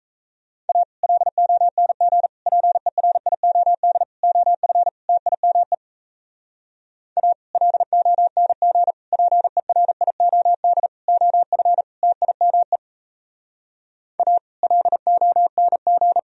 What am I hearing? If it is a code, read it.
Morse code, heard as a keyed tone, about 39 wpm, 700 Hz. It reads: A LONG PERIOD OF TIME A LONG PERIOD OF TIME A LONG